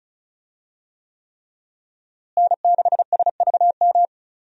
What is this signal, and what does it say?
Morse code, heard as a keyed tone, about 36 wpm, 700 Hz. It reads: N6SVM
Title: USA Call Signs - 5 Characters - Repeated 3x - 35wpm